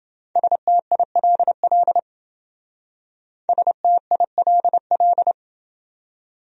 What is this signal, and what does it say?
Morse code, heard as a keyed tone, about 28 wpm, 700 Hz. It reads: STILL STILL